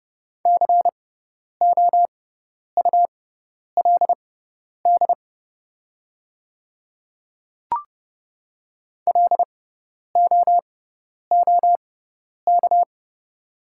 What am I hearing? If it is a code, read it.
Morse code, heard as a keyed tone, about 31 wpm, spaced as for 10 wpm, 700 Hz. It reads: COULD LOOK
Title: 6x Character Spacing - Top 200 Words - 30wpm